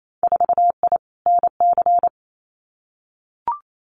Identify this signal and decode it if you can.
Morse code, heard as a keyed tone, about 28 wpm, 700 Hz. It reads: 4I NC
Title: Winter Field Day Exchanges 28wpm